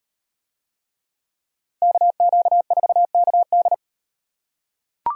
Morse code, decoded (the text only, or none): KQ4KD